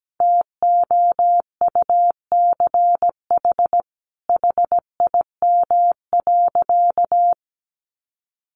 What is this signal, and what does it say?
Morse code, heard as a keyed tone, about 17 wpm, 700 Hz. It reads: TOUCH HIM.
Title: Sentences from Top 1200 Words 17wpm